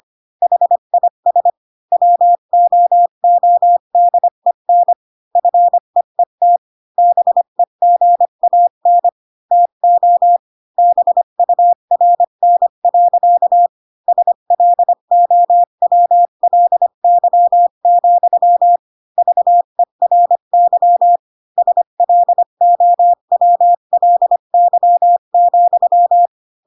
Morse code, heard as a keyed tone, about 25 wpm, 700 Hz. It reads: HIS WOODEN FEET BEGAN TO BURN. SLOWLY, VERY SLOWLY,